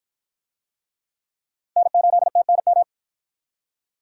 Morse code, heard as a keyed tone, about 51 wpm, 700 Hz. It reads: N8TNM